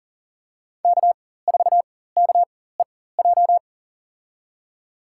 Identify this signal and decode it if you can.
Morse code, heard as a keyed tone, about 37 wpm, spaced as for 18 wpm, 700 Hz. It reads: K4KEJ
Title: USA Call Signs - 5 Characters - 4x Character Spacing - 40wpm